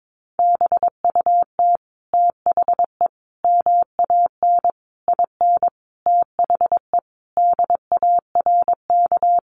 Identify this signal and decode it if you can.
Morse code, heard as a keyed tone, about 22 wpm, 700 Hz. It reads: BUT THE MAN IN THE DARK
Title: Sentences from Top 600 Words 22wpm